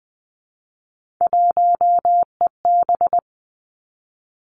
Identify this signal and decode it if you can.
Morse code, heard as a keyed tone, about 20 wpm, 700 Hz. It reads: 1EB
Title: Three Challenging Letter-Number Combinations 20wpm